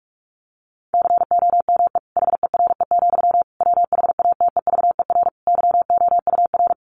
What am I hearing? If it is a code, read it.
Morse code, heard as a keyed tone, about 44 wpm, 700 Hz. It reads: COME HERE, WHATEVER YOUR
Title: Sentences from Top 1100 Words 45wpm